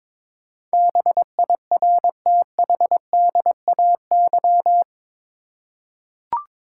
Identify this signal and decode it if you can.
Morse code, heard as a keyed tone, about 22 wpm, 700 Hz. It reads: BIRTHDAY
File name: practice / 22wpm / Eight Letter Words - 22wpm